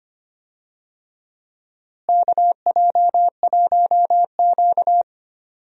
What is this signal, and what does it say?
Morse code, heard as a keyed tone, about 25 wpm, 700 Hz. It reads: KJ1Q
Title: USA Call Signs - 4 Characters - Repeated 2x - 25wpm